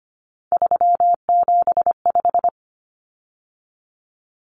Morse code, heard as a keyed tone, about 25 wpm, 700 Hz. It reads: 375
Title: Common Cut Numbers - 3-Digit 25wpm